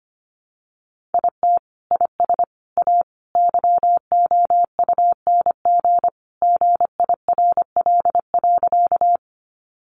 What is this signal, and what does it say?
Morse code, heard as a keyed tone, about 25 wpm, 700 Hz. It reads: IT IS A YOUNG GIRL.
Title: Sentences from Top 400 Words 25wpm